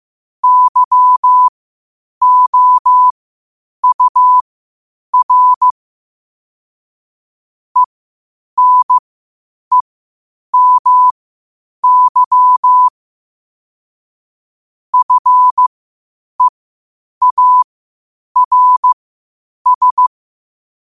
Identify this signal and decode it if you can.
Morse code, heard as a keyed tone, about 15 wpm, spaced as for 8 wpm, 1000 Hz. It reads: YOUR ENEMY FEARS
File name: qotdfiles / 8wpm